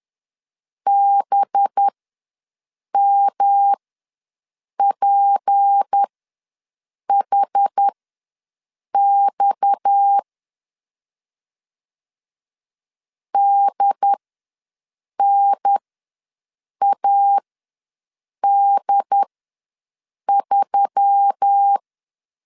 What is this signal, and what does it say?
Morse code, heard as a keyed tone, about 10 wpm, spaced as for 6 wpm, 800 Hz. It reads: BMPHX DNAD3
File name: LEZIONI / 6 PKX0 / 6 PKX ws 10wpm 3706